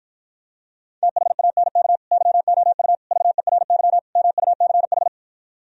Morse code, heard as a keyed tone, about 52 wpm, 700 Hz. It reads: THANK YOU VERY MUCH